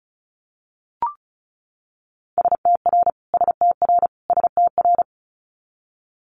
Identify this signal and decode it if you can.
Morse code, heard as a keyed tone, about 36 wpm, 700 Hz. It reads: STR STR STR
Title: Multi-letter Phonemes Repeated 3x 35wpm